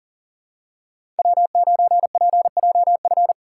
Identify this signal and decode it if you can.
Morse code, heard as a keyed tone, about 40 wpm, 700 Hz. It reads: W9PJF
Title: USA Call Signs - 5 Characters - 4x Character Spacing - 40wpm